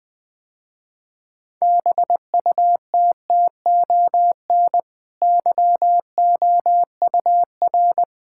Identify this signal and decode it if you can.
Morse code, heard as a keyed tone, about 20 wpm, 700 Hz. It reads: BUTTON YOUR
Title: Easy Sentences with Hints - 20wpm